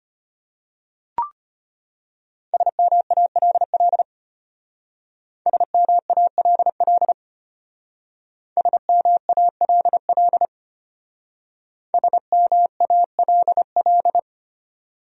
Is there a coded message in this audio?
Yes: Morse code from 2.5 to 14.2 s, heard as a keyed tone, about 32 wpm, 700 Hz. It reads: SMALL SMALL SMALL SMALL